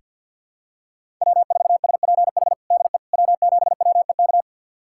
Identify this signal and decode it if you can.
Morse code, heard as a keyed tone, about 50 wpm, 690 Hz. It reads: W4SPH DE W7WEK
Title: USA Callsign DE Callsign - 5 Characters - 50wpm